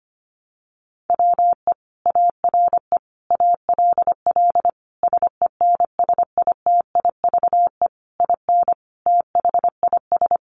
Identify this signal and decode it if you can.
Morse code, heard as a keyed tone, about 25 wpm, 700 Hz. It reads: WE ARE ALL SENSITIVE IN THIS
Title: Sentences from Top 1400 Words 25wpm